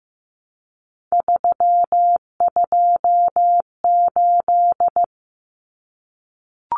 Morse code, heard as a keyed tone, about 15 wpm, 700 Hz. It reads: 328
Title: Common Cut Numbers - 3-Digit 15wpm